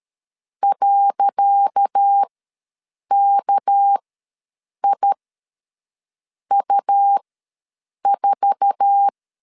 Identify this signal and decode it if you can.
Morse code, heard as a keyed tone, about 13 wpm, spaced as for 6 wpm, 800 Hz. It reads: .KIU4
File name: LEZIONI / 10 SIGNS / 10C DOT ws 12wpm 3548